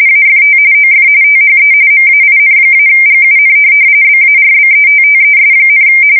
RTTY, about 45.45 baud. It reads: AFT OF THE MAIN-GUN BARBETTES,
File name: fsk170 / USS-Missouri